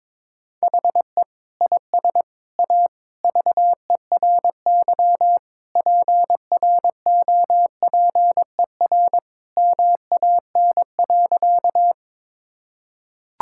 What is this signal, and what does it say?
Morse code, heard as a keyed tone, about 22 wpm, 700 Hz. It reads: HE IS A VERY PROPER MAN.
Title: Sentences from Top 1000 Words - Review Part 5 - 22wpm